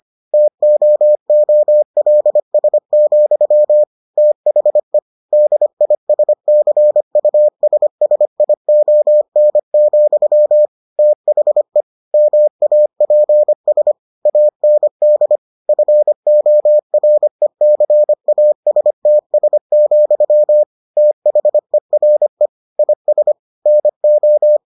Morse code, heard as a keyed tone, about 25 wpm, 600 Hz. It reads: TOOLS, THE DISCUSSION, THE MAPS AND FORECASTS, THERE IS NO